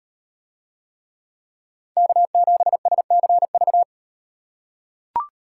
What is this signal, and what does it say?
Morse code, heard as a keyed tone, about 35 wpm, 700 Hz. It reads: K7SCV